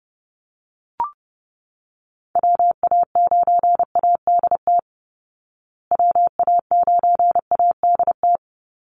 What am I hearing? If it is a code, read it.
Morse code, heard as a keyed tone, about 30 wpm, 700 Hz. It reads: WA9ADT WA9ADT
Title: Call Signs Repeated 3x - 30wpm